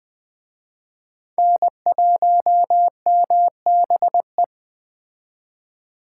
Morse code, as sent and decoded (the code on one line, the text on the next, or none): -. .---- -- -... .
N1MBE